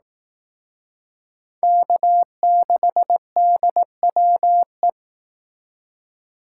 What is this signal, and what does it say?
Morse code, heard as a keyed tone, about 19 wpm, 700 Hz. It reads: K6DWE
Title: Speed Racing - USA 1x3 Call Signs 15wpm